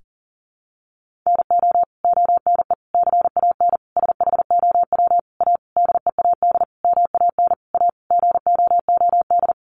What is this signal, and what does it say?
Morse code, heard as a keyed tone, about 40 wpm, 700 Hz. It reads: NO ONE CAN SHOW A DEAD MAN A GOOD
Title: Sentences from Top 1500 Words 40wpm